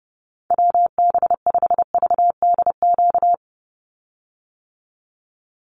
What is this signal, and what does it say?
Morse code, heard as a keyed tone, about 29 wpm, 700 Hz. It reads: WB5VDQ
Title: USA Call Signs 30wpm